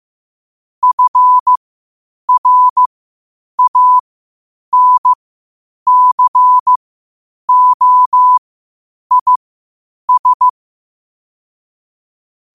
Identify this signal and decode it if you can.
Morse code, heard as a keyed tone, about 15 wpm, spaced as for 8 wpm, 1000 Hz. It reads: FRANCOIS
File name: qotdfiles / 8wpm